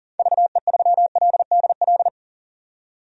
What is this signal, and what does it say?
Morse code, heard as a keyed tone, about 39 wpm, 700 Hz. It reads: VE3LDL
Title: DXCC-100 Call Signs - 6 Characters - 40wpm